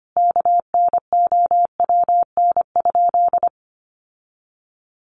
Morse code, heard as a keyed tone, about 25 wpm, 700 Hz. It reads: KNOWN?
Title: Sentences from Top 1200 Words 25wpm